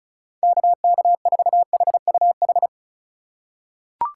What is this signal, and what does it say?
Morse code, heard as a keyed tone, about 34 wpm, 700 Hz. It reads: KK4HUH
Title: USA Call Signs - 6 Characters - 2x Character Spacing - 35wpm